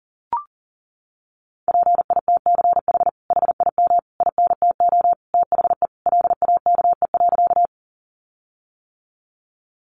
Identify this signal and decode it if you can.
Morse code, heard as a keyed tone, about 37 wpm, 700 Hz. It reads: PITCH HIM INTO THE LAKE.
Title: Sentences from Top 1300 Words 40wpm